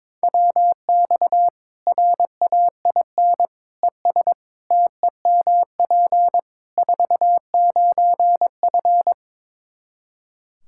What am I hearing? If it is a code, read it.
Morse code, heard as a keyed tone, about 22 wpm, 700 Hz. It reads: WX RAIN ES TEMP 49F
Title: Practice QSO - Exchange 2 - Info - 22wpm